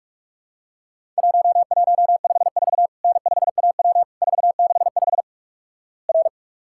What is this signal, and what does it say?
Morse code, heard as a keyed tone, about 46 wpm, 700 Hz. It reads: 1154 N5AW 465 R
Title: Worked All Europe DX Contest - QTC Exchange - 45wpm